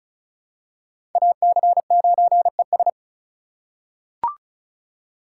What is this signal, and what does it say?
Morse code, heard as a keyed tone, about 36 wpm, 700 Hz. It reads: AC9ES
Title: USA Call Signs - 5 Characters - 2x Character Spacing - 35wpm